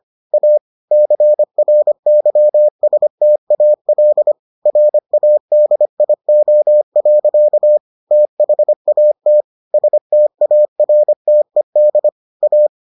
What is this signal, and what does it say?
Morse code, heard as a keyed tone, about 25 wpm, 600 Hz. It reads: A CRYSTAL RADIO. THAT STARTED A